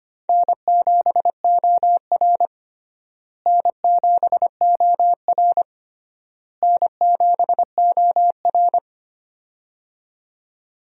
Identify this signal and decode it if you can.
Morse code, heard as a keyed tone, about 25 wpm, 700 Hz. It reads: N7OR N7OR N7OR
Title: USA Call Signs - 4 Characters - Repeated 3x - 25wpm